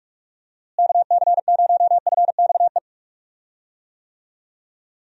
Morse code, heard as a keyed tone, about 42 wpm, 700 Hz. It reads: KC0FXE
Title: USA Call Signs - 6 Characters - Repeated 2x - 45wpm